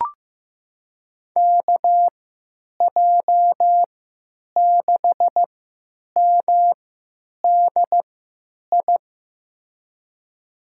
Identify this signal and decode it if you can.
Morse code, heard as a keyed tone, about 15 wpm, spaced as for 9 wpm, 700 Hz. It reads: KJ6MDI
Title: USA Call Signs - 6 Characters - 3x Character Spacing - 15wpm